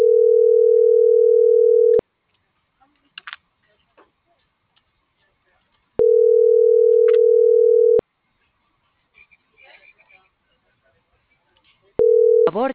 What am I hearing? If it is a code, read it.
Ringback tone (North America), 440 + 480 Hz.